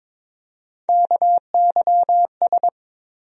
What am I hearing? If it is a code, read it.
Morse code, heard as a keyed tone, about 21 wpm, 700 Hz. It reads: KYS
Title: USA Call Sign Postfixes - 3 Letters 22wpm